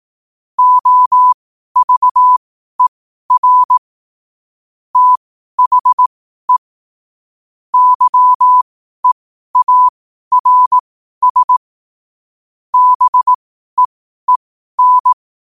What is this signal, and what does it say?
Morse code, heard as a keyed tone, about 18 wpm, spaced as for 12 wpm, 1000 Hz. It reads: OVER THE YEARS BEEN
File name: qotdfiles / 12wpm